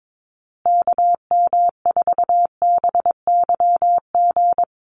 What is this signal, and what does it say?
Morse code, heard as a keyed tone, about 22 wpm, 700 Hz. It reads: KM4BYG
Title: USA Call Signs Repeated 3x 22wpm